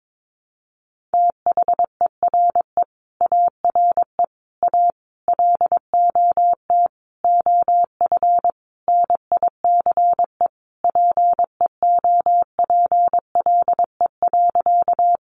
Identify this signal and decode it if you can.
Morse code, heard as a keyed tone, about 22 wpm, 700 Hz. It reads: THERE ARE A LOT OF NICE PEOPLE.
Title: Sentences from Top 1100 Words 22wpm